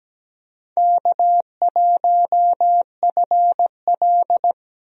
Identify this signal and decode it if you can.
Morse code, heard as a keyed tone, about 17 wpm, 700 Hz. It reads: K1FL
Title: USA Call Signs - 4 Characters - 17wpm